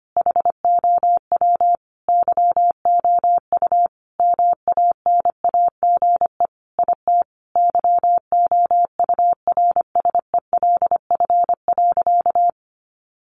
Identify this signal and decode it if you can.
Morse code, heard as a keyed tone, about 25 wpm, 700 Hz. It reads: HOW YOU MANAGE IT YOURSELF.
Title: Sentences from Top 1400 Words 25wpm